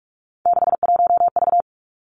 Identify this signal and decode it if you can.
Morse code, heard as a keyed tone, about 46 wpm, 700 Hz. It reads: 61V
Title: Three Challenging Letter-Number Combinations 45wpm